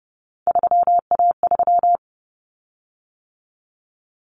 Morse code, heard as a keyed tone, about 30 wpm, 700 Hz. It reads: 3A3
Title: Common Cut Numbers - 3-Digit 30wpm